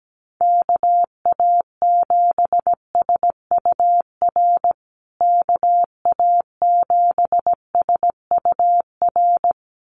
Morse code, heard as a keyed tone, about 17 wpm, 700 Hz. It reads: KA7SUR KA7SUR
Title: USA Call Signs Repeated 2x 17wpm